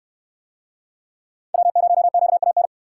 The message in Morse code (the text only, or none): W0QAN